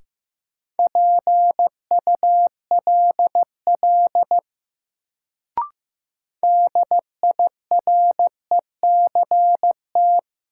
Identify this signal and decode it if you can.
Morse code, heard as a keyed tone, about 15 wpm, 700 Hz. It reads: PULL DIRECT
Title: Sentences from Top 400 Words - New Word Drill - 15wpm